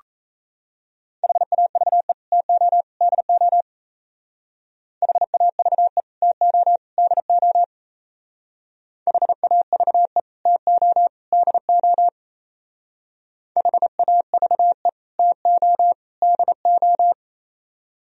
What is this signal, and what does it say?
Morse code, heard as a keyed tone, about 34 wpm, 700 Hz. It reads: HAVE TO DO HAVE TO DO HAVE TO DO HAVE TO DO
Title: Speed-Racing - Sets of 3 Words 28wpm